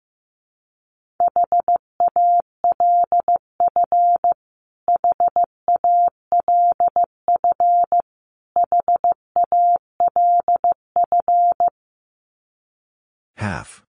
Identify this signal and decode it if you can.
Morse code, heard as a keyed tone, about 15 wpm, 700 Hz. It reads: HALF HALF HALF
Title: Top 500 Words Repeated 15wpm